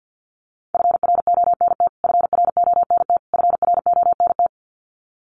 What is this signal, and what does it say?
Morse code, heard as a keyed tone, about 50 wpm, 700 Hz. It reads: FRONT FRONT FRONT
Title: Top 500 Words Repeated 50wpm